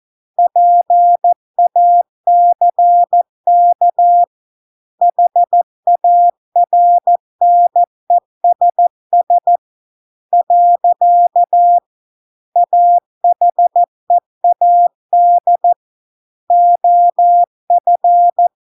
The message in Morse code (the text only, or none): PACK HARNESS . AHEAD OF